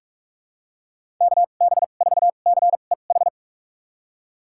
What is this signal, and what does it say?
Morse code, heard as a keyed tone, about 41 wpm, spaced as for 31 wpm, 700 Hz. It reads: KB4CEH
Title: USA Call Signs - 6 Characters - 2x Character Spacing - 45wpm